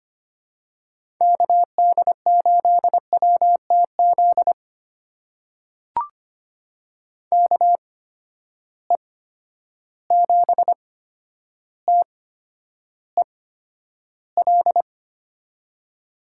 Morse code, heard as a keyed tone, about 25 wpm, spaced as for 11 wpm, 700 Hz. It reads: KD8WTZ KE7TEL
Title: USA Call Signs - 8x Character Spacing - 25wpm